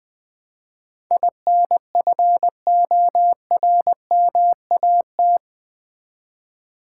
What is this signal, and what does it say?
Morse code, heard as a keyed tone, about 20 wpm, 700 Hz. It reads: INFORMAT
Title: Build A Word - 4 Letters At A Time - 11-Letter Words - 20wpm